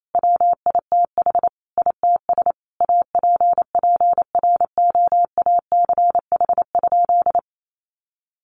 Morse code, heard as a keyed tone, about 28 wpm, 700 Hz. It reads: WITH ITS APPROACH?
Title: Sentences from Top 1500 Words 28wpm